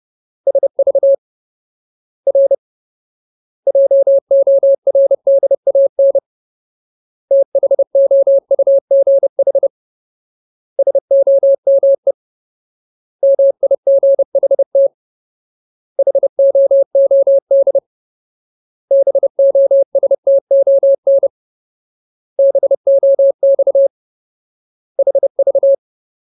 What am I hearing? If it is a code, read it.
Morse code, heard as a keyed tone, about 30 wpm, 550 Hz. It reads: SV R JORDAN THOUGH SOME MIGHT HOOD BOSTON BOX HV
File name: Ultimate Word List / 30WPM Ultimate